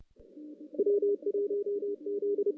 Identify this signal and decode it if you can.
Morse code, heard as a keyed tone, about 31 wpm, 410 Hz. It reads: W1Z
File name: 2016 JAN CW NAQP / 2016 Jan naqp k5ot - 0-5